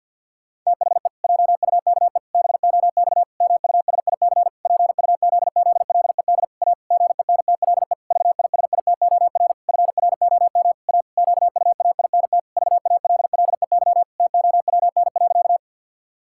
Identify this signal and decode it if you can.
Morse code, heard as a keyed tone, about 51 wpm, 700 Hz. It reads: THE JUKE BOX MUSIC PUZZLED A GENTLE VISITOR FROM A QUAINT VALLEY TOWN.
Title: Pangram Challenge v2 - 50wpm